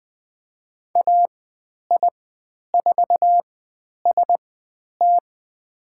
Morse code, heard as a keyed tone, about 20 wpm, spaced as for 10 wpm, 700 Hz. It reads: AI4ST
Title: 102 Lesson 17g - Letter u - Call Signs 20wpm